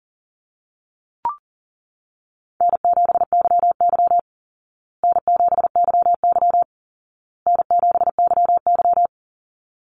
Morse code, heard as a keyed tone, about 40 wpm, 700 Hz. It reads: N7YY N7YY N7YY
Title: Call Signs Repeated 2x - 40wpm